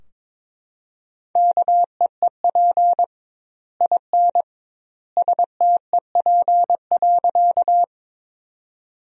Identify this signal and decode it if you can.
Morse code, heard as a keyed tone, about 22 wpm, 700 Hz. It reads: KEEP IN STEP.
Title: 2x Word-Spacing - Sentences from Top 400 Words - 22wpm